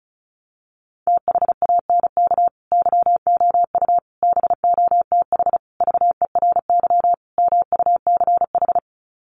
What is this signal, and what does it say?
Morse code, heard as a keyed tone, about 34 wpm, 700 Hz. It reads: THANK YOU BOTH VERY MUCH
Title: Sets of 5 Words 35wpm